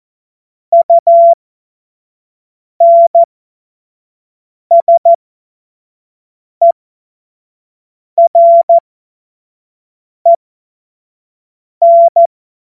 Morse code, heard as a keyed tone, about 14 wpm, spaced as for 5 wpm, 680 Hz. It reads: UNSEREN
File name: Morsekurs / Witze / Witz 05 Dezember 2023 25 F70